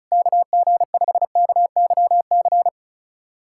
Morse code, heard as a keyed tone, about 36 wpm, 700 Hz. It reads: KG5KYC
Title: USA Callsign DE Callsign - 6 Characters - 35wpm